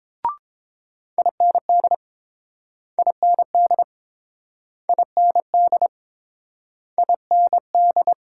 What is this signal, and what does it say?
Morse code, heard as a keyed tone, about 30 wpm, 700 Hz. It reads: IND IND IND IND